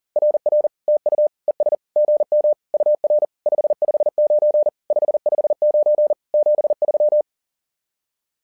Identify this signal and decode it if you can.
Morse code, heard as a keyed tone, about 42 wpm, 600 Hz. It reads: RR TU ES GM UR 559 559 73
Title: SOTA QSO Practice 40wpm